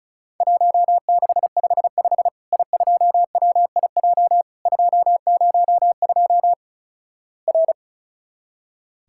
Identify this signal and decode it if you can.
Morse code, heard as a keyed tone, about 36 wpm, 700 Hz. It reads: 1655 I2WIJ 202 R
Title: Worked All Europe DX Contest - QTC Exchange - 35wpm